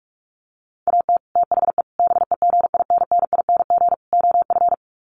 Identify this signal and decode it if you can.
Morse code, heard as a keyed tone, about 44 wpm, 700 Hz. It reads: AT THE BEGINNING OF
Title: Sets of 4 Words 45wpm